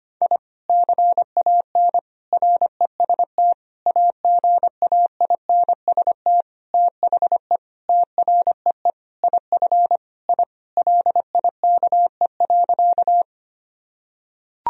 Morse code, heard as a keyed tone, about 25 wpm, 700 Hz. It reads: I CAN REST AGAINST THE TREE IF I LIKE.
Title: Sentences from Top 1000 Words - Review Part 1 - 25wpm